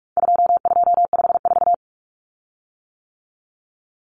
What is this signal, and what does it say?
Morse code, heard as a keyed tone, about 44 wpm, 700 Hz. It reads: 2254
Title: Common Cut Numbers - 4-Digit 45wpm